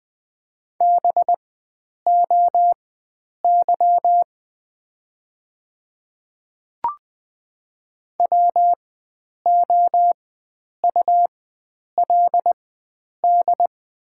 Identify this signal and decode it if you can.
Morse code, heard as a keyed tone, about 20 wpm, spaced as for 9 wpm, 700 Hz. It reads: BOY WOULD
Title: 4x Character Spacing - Top 300 Words - 20wpm